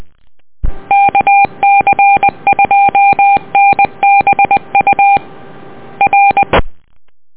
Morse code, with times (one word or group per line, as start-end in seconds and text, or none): 0.9-5.2 KC2NBU
6.0-6.4 R